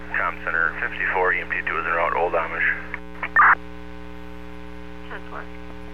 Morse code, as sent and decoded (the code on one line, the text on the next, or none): none